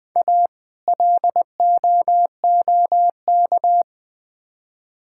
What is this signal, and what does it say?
Morse code, heard as a keyed tone, about 20 wpm, 700 Hz. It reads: A LOOK
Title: Build A Phrase - 3-Word Phrases - 20wpm